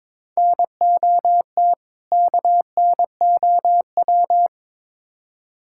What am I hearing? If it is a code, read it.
Morse code, heard as a keyed tone, about 23 wpm, 700 Hz. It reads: NOT KNOW
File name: practice / 22wpm / Build A Phrase - 5-Word Phrases - 22wpm